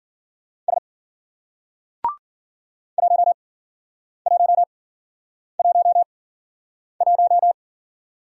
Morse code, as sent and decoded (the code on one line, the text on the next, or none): ... / .---- .---- .---- .----
S 1111